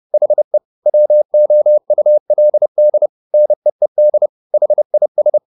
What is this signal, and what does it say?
Morse code, heard as a keyed tone, about 30 wpm, 600 Hz. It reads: HE WOULD NEED HIS
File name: the old man and the sea / the old man and the sea 1 30wpm